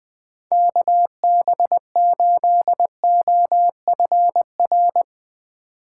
Morse code, heard as a keyed tone, about 20 wpm, 700 Hz. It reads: KB8OFR
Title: USA Call Signs - 6 Characters - 4x Character Spacing - 20wpm